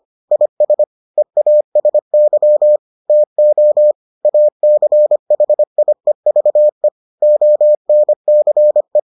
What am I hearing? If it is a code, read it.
Morse code, heard as a keyed tone, about 25 wpm, 600 Hz. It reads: IS EASY TO ACHIEVE ONCE